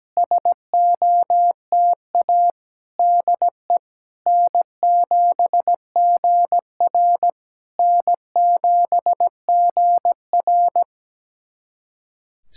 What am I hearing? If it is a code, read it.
Morse code, heard as a keyed tone, about 17 wpm, 690 Hz. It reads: SOTA DE N7GR N7GR
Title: SOTA QSO Practice 17wpm